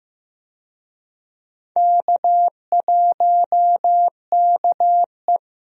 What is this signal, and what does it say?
Morse code, heard as a keyed tone, about 15 wpm, 700 Hz. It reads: K1KE
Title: USA Call Signs - 4 Characters - Repeated 3x - 15wpm